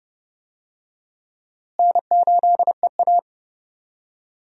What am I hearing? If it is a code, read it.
Morse code, heard as a keyed tone, about 30 wpm, 700 Hz. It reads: N8EA
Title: USA Call Signs - 4 Characters - Repeated 3x - 30wpm